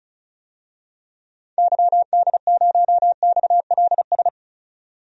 Morse code, heard as a keyed tone, about 36 wpm, 700 Hz. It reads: YD0XLS
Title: DXCC-100 Call Signs - 6 Characters - 35wpm